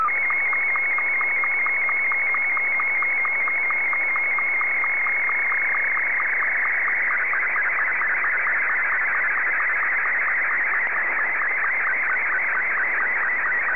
SSTV (slow-scan television), Martin 2.
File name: sstv / M2-VK3DFL